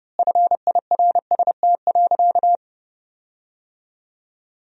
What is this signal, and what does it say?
Morse code, heard as a keyed tone, about 30 wpm, 700 Hz. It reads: FIRST.
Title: 3x Word-Spacing - Sentences from Top 400 Words - 30wpm